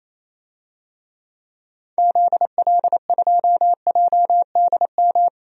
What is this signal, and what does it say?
Morse code, heard as a keyed tone, about 28 wpm, 700 Hz. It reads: ZL2JDM
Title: DXCC-100 Call Signs - 6 Characters - 28wpm